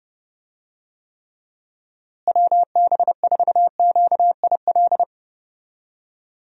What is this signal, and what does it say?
Morse code, heard as a keyed tone, about 32 wpm, 700 Hz. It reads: WB4QIL
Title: USA Call Signs - 6 Characters - Repeated 3x - 30wpm